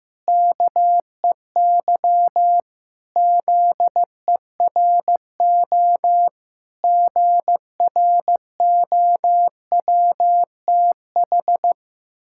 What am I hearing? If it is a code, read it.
Morse code, heard as a keyed tone, about 15 wpm, 700 Hz. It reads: KEY ZERO GROWTH
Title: Pangram Challenge v2 - 15wpm